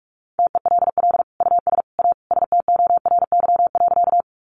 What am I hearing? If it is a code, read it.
Morse code, heard as a keyed tone, about 44 wpm, 700 Hz. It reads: TELL US A STORY.
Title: Sentences from Top 300 Words 45wpm